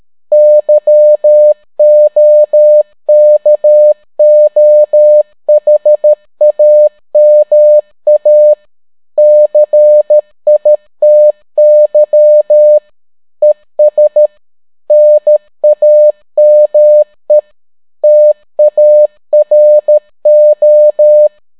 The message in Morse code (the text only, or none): YOKOHAMA CITY ES NAME TARO